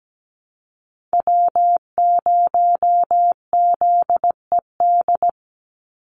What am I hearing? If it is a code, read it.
Morse code, heard as a keyed tone, about 17 wpm, 700 Hz. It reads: W0ZED
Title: USA Call Signs 17wpm